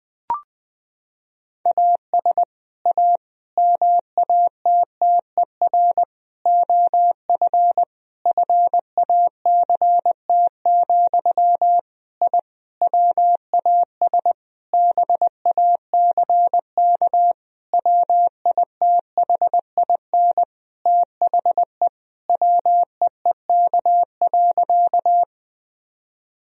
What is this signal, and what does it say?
Morse code, heard as a keyed tone, about 20 wpm, 700 Hz. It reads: AS A MATTER OF FACT, I WAS BACK WITHIN THE WEEK.
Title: Sentences from Top 1000 Words - Review Part 4 - 20wpm